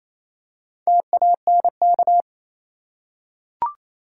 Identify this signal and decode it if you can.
Morse code, heard as a keyed tone, about 28 wpm, 700 Hz. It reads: TANK
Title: Four Letter Words - 28wpm